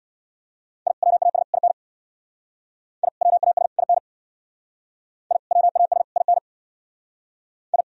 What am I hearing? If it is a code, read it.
Morse code, heard as a keyed tone, about 60 wpm, 700 Hz. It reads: I WAS IN I WAS IN I WAS IN I
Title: Speed-Racing - Sets of 3 Words 50wpm